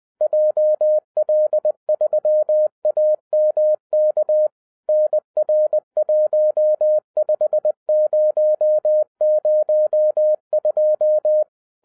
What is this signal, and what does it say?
Morse code, heard as a keyed tone, about 20 wpm, 610 Hz. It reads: JL3AMK NR15002